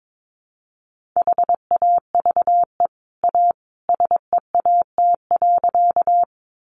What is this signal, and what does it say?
Morse code, heard as a keyed tone, about 22 wpm, 700 Hz. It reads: HAVE A SEAT.
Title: Sentences from Top 1500 Words 22wpm